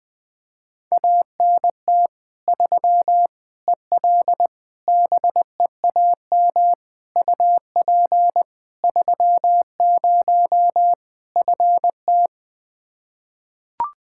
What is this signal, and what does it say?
Morse code, heard as a keyed tone, about 20 wpm, 700 Hz. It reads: ANT 3 EL BEAM UP 30 FT
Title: Practice QSO - Exchange 2 - Info - 20wpm